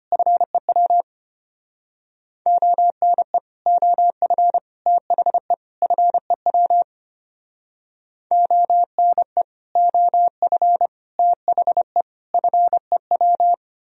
Morse code, heard as a keyed tone, about 32 wpm, 700 Hz. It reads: FEW ONE OF THE FEW ONE OF THE FEW